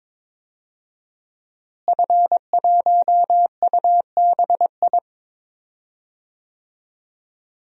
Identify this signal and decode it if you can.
Morse code, heard as a keyed tone, about 22 wpm, 700 Hz. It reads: F1UBI
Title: DXCC-100 Call Signs - 5 Characters - Repeated 3x - 22wpm